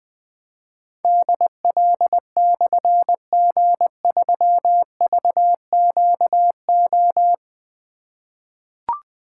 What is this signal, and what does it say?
Morse code, heard as a keyed tone, about 20 wpm, 700 Hz. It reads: DL/G3VQO
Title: Difficult Call Signs 20wpm